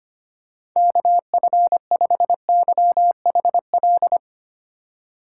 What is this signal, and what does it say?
Morse code, heard as a keyed tone, about 25 wpm, 700 Hz. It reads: KF5YHL
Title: USA Call Signs - 6 Characters - 2x Character Spacing - 25wpm